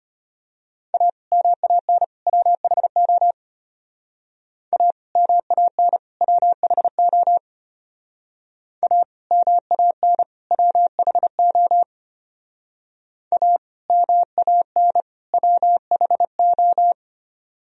Morse code, heard as a keyed tone, about 33 wpm, 700 Hz. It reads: A MAN WHO A MAN WHO A MAN WHO A MAN WHO